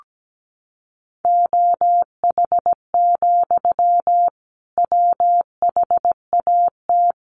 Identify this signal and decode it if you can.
Morse code, heard as a keyed tone, about 17 wpm, 700 Hz. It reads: OH, WHAT
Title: Sentences from Top 1500 Words 17wpm